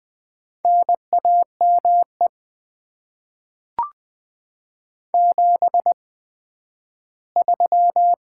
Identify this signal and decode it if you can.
Morse code, heard as a keyed tone, about 20 wpm, spaced as for 12 wpm, 700 Hz. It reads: NAME 73
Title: Top 20 QSO Elements - 8x Character Spacing - 20wpm